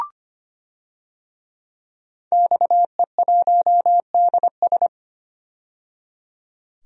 Morse code, heard as a keyed tone, about 25 wpm, 700 Hz. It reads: XE1DS
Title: DXCC-100 Call Signs - 5 Characters - 25wpm